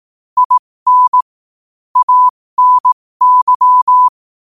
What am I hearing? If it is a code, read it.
Morse code, heard as a keyed tone, about 18 wpm, spaced as for 15 wpm, 1000 Hz. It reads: IN ANY
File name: qotdfiles / 15wpm